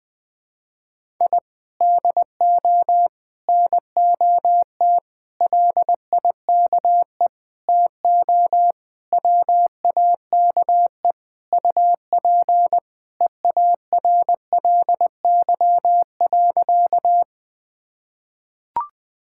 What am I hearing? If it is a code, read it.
Morse code, heard as a keyed tone, about 20 wpm, 700 Hz. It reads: I DO NOT LIKE TO WAKE UP EARLY.
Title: Sentences with 5-Word Phrases - 20wpm